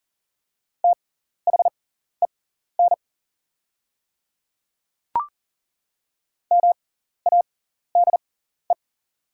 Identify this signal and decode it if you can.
Morse code, heard as a keyed tone, about 39 wpm, spaced as for 14 wpm, 700 Hz. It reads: THEN MADE
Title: 6x Character Spacing - Top 200 Words - 40wpm